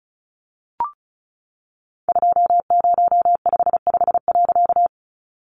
Morse code, heard as a keyed tone, about 34 wpm, 700 Hz. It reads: 2055.
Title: A Sound Of Thunder 35wpm